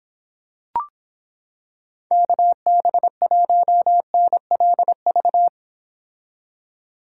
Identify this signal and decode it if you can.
Morse code, heard as a keyed tone, about 26 wpm, 700 Hz. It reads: KB1NLV